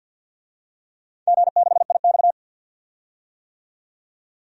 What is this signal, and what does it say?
Morse code, heard as a keyed tone, about 47 wpm, 700 Hz. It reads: G6IX
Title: DXCC-100 Call Signs - 4 Characters - 50wpm